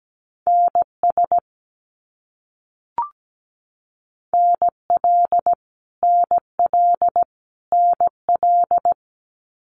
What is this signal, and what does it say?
Morse code, heard as a keyed tone, about 17 wpm, 700 Hz. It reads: NS NL NL NL
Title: Canadian Provinces and Territories - Repeated 3x - 17wpm